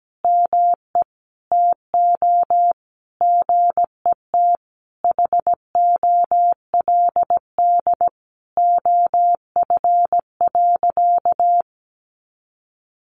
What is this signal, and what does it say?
Morse code, heard as a keyed tone, about 17 wpm, 700 Hz. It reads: ME TO GET HOLD OF.
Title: A Sound Of Thunder 17wpm